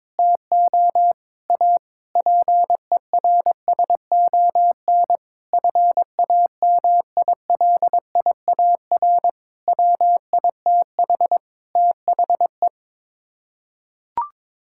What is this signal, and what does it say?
Morse code, heard as a keyed tone, about 22 wpm, 700 Hz. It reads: TO A PERSON FAMILIAR WITH THE
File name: practice / 22wpm / Sets of 7 Words - 22wpm